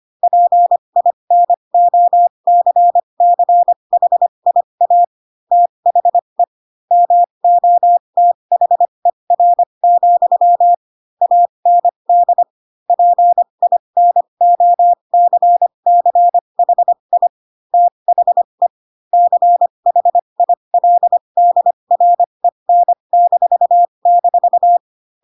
Morse code, heard as a keyed tone, about 25 wpm, spaced as for 22 wpm, 700 Hz. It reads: PINOCCHIA THE MOTHER, AND PINOCCHI THE CHILDREN--
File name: Pinocchio / Pinocchio ch3 18 25 ik1hge